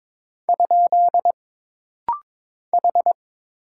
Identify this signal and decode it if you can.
Morse code, heard as a keyed tone, about 22 wpm, 700 Hz. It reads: ? H
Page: This rapid-fire timing is geared to push your ICR (Instant Character Recognition) skills.